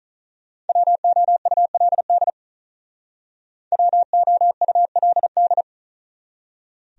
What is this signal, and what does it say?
Morse code, heard as a keyed tone, about 46 wpm, 700 Hz. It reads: WOULD WOULD